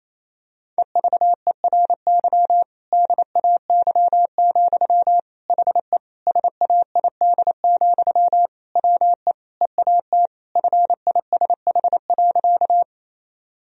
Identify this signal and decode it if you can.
Morse code, heard as a keyed tone, about 28 wpm, 700 Hz. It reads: EVERY DAY, HE SAID, WE EAT FISH.
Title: Sentences from Top 1000 Words - Review Part 1 - 28wpm